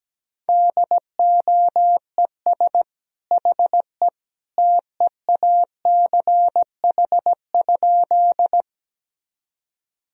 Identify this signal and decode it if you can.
Morse code, heard as a keyed tone, about 17 wpm, 700 Hz. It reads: DOES HE TEACH?
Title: Sentences from Top 1000 Words - Review Part 2 - 17wpm